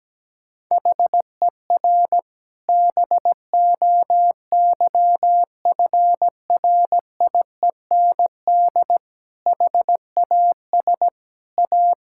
Morse code, heard as a keyed tone, about 17 wpm, 700 Hz. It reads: HER BOYFRIEND HAS A
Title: Sentences from Top 1000 Words - Review Part 5 - 17wpm